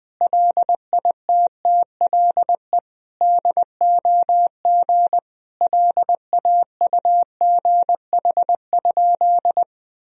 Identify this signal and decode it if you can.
Morse code, heard as a keyed tone, about 20 wpm, 700 Hz. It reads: LITTLE DOG LAUGH?
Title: Sentences from Top 500 Words and Abbreviations 20wpm